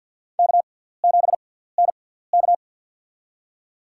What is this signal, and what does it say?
Morse code, heard as a keyed tone, about 53 wpm, spaced as for 17 wpm, 700 Hz. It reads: K7NK
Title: USA Call Signs - 4 Characters - 6x Character Spacing - 50wpm